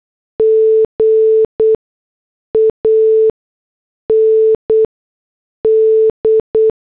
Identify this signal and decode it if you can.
Morse code, heard as a keyed tone, about 8 wpm, spaced as for 6 wpm, 440 Hz. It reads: G AND